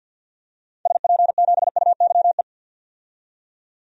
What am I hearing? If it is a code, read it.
Morse code, heard as a keyed tone, about 51 wpm, 700 Hz. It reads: SP7UYE
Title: DXCC-100 Call Signs - 6 Characters - 50wpm